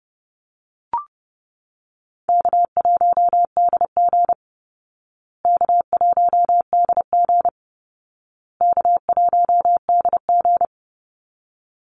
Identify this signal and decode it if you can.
Morse code, heard as a keyed tone, about 30 wpm, 700 Hz. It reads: K1DG K1DG K1DG
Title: Easy Call Signs Repeated 3x - 30wpm